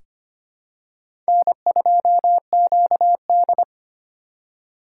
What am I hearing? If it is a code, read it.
Morse code, heard as a keyed tone, about 25 wpm, 700 Hz. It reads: N2QD
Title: USA Call Signs - 3x Character Spacing - 25wpm